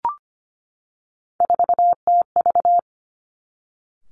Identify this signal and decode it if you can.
Morse code, heard as a keyed tone, about 25 wpm, 700 Hz. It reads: 4TV
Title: Three Challenging Letter-Number Combinations 25wpm